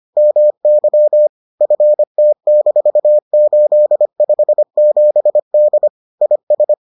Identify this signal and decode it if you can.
Morse code, heard as a keyed tone, about 25 wpm, 600 Hz. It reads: MY FT-857D IS